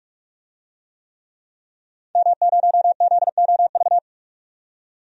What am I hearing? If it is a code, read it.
Morse code, heard as a keyed tone, about 47 wpm, 700 Hz. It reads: M0ZOV
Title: DXCC-100 Call Signs - 5 Characters - 45wpm